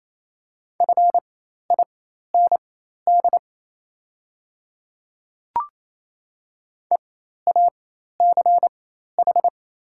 Morse code, heard as a keyed tone, about 28 wpm, spaced as for 13 wpm, 700 Hz. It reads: FIND EACH
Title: 4x Character Spacing - Top 100 Words - 28wpm